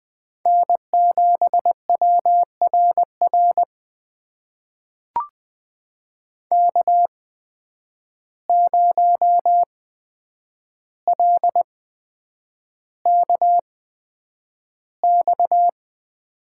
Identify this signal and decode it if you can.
Morse code, heard as a keyed tone, about 20 wpm, spaced as for 9 wpm, 700 Hz. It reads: N7WRR K0LKX
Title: USA Call Signs - 5 Characters - 8x Character Spacing - 20wpm